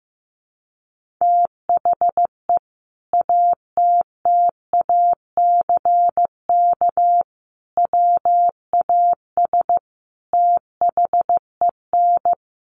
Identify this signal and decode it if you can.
Morse code, heard as a keyed tone, about 15 wpm, 700 Hz. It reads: THE ATTACK WAS THEN
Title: Sentences from Top 1500 Words 15wpm